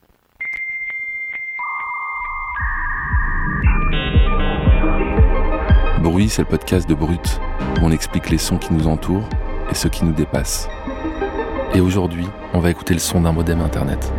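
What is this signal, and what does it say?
Dial-up modem handshake, with speech.
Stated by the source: Dial-up internet modem connection sound.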